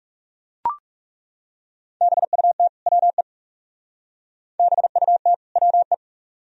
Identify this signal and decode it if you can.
Morse code, heard as a keyed tone, about 35 wpm, 700 Hz. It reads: BUT WE BUT WE
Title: Speed-Racing - Sets of 2 Words 30wpm